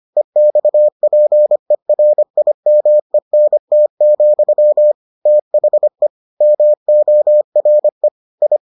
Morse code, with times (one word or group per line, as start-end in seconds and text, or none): 0.2-4.9 EXPERIMENT,
5.3-6.1 THE
6.4-8.1 MORE
8.4-8.6 I